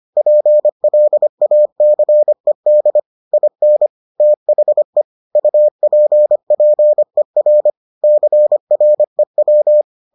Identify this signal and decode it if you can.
Morse code, heard as a keyed tone, about 25 wpm, 600 Hz. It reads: PLACED IN THE UPPER CREW